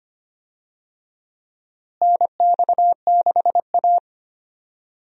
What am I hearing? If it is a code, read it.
Morse code, heard as a keyed tone, about 25 wpm, 700 Hz. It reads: NX6A
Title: USA Call Signs - 4 Characters - Repeated 3x - 25wpm